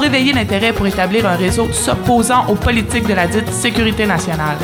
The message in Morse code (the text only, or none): none